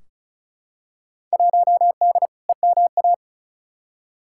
Morse code, heard as a keyed tone, about 34 wpm, 700 Hz. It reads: 1D EMA
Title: Field Day Exchanges - 35wpm